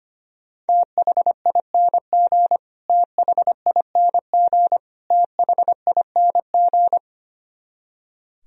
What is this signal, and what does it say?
Morse code, heard as a keyed tone, about 25 wpm, 700 Hz. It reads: THING THING THING
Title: Top 100 Words Repeated 25wpm